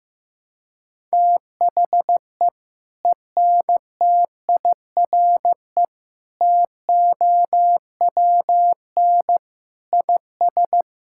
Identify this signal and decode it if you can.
Morse code, heard as a keyed tone, about 15 wpm, 700 Hz. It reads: THE ENTIRE TOWN IS
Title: Sentences from Top 1000 Words - Review Part 5 - 15wpm